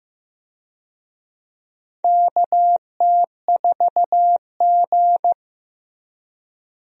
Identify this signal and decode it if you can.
Morse code, heard as a keyed tone, about 15 wpm, 700 Hz. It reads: KT4G
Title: USA Call Signs - 4 Characters - Repeated 3x - 15wpm